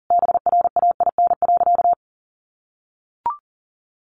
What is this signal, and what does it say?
Morse code, heard as a keyed tone, about 40 wpm, 700 Hz. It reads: BRAIN.
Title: Sentences from Top 1100 Words 40wpm